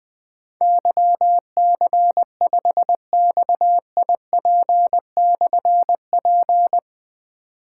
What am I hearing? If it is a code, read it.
Morse code, heard as a keyed tone, about 20 wpm, 700 Hz. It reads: YC5XIP/P
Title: Difficult Call Signs 20wpm